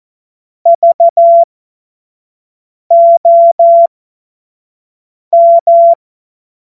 Morse code, heard as a keyed tone, about 14 wpm, spaced as for 5 wpm, 680 Hz. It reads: VOM
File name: Morsekurs / Witze / Witz 13. Juni 2023 25 F70